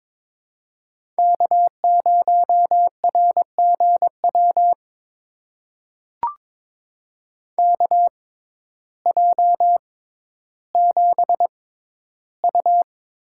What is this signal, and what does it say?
Morse code, heard as a keyed tone, about 22 wpm, spaced as for 12 wpm, 700 Hz. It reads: K0RGW KJ7U
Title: USA Call Signs - 5 Characters - 6x Character Spacing - 22wpm